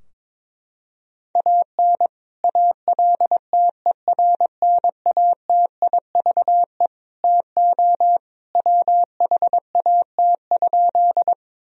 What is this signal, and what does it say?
Morse code, heard as a keyed tone, about 22 wpm, 700 Hz. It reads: AN ALTERNATIVE TO WHAT?
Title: Sentences from Top 1000 Words - Review Part 4 - 22wpm